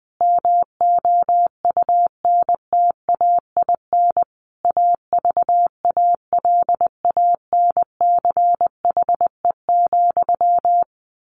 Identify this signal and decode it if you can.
Morse code, heard as a keyed tone, about 20 wpm, 700 Hz. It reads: MOUNTAIN AVALANCHE,
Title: A Sound Of Thunder 20wpm